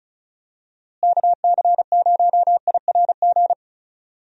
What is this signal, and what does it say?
Morse code, heard as a keyed tone, about 36 wpm, 700 Hz. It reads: KC0IRG
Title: USA Call Signs - 6 Characters - Repeated 2x - 35wpm